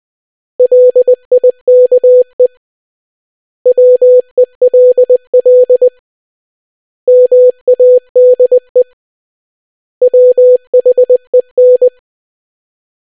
Morse code, heard as a keyed tone, about 20 wpm, 500 Hz. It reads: LIKE WELL MADE WHEN